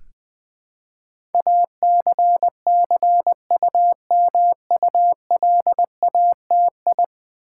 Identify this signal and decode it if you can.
Morse code, heard as a keyed tone, about 20 wpm, 700 Hz. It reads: ACCUMULATI
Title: Build A Word - 5 Letters At A Time - 12-Letter Words - 20wpm